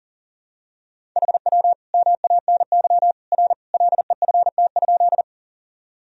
Morse code, heard as a keyed tone, about 40 wpm, 700 Hz. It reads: HW MANY R LEFT?
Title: Sentences from Top 300 Words and Abbreviations 40wpm